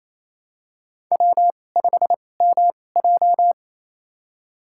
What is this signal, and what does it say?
Morse code, heard as a keyed tone, about 28 wpm, spaced as for 20 wpm, 700 Hz. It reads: W5MJ
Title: USA Call Signs - 4 Characters - 2x Character Spacing - 28wpm